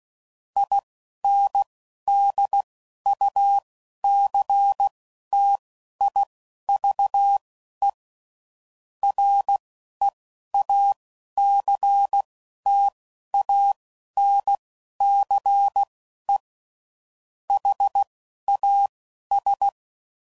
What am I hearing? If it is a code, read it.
Morse code, heard as a keyed tone, about 16 wpm, spaced as for 11 wpm, 800 Hz. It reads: INDUCTIVE REACTANCE HAS
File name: CW Code Groups @10WPM / Technical Subjects / XL